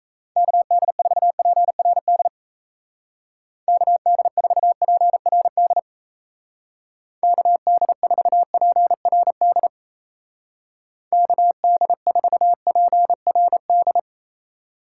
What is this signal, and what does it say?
Morse code, heard as a keyed tone, about 37 wpm, 700 Hz. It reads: KD4PRD KD4PRD KD4PRD KD4PRD